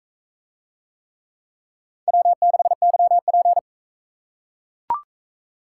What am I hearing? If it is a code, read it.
Morse code, heard as a keyed tone, about 42 wpm, 700 Hz. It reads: W6YP